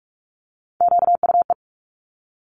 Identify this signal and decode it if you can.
Morse code, heard as a keyed tone, about 46 wpm, 700 Hz. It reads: QUE
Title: Top Three Letter Combinations in Words 45wpm